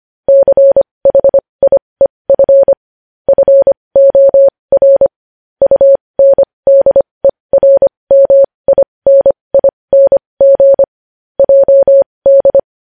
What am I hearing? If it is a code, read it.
Morse code, heard as a keyed tone, about 25 wpm, spaced as for 20 wpm, 550 Hz. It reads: CHIEF FOR UNDERMINING JD
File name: podcast / News headlines-1x-25x20